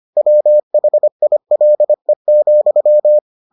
Morse code, heard as a keyed tone, about 25 wpm, 600 Hz. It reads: WHILE,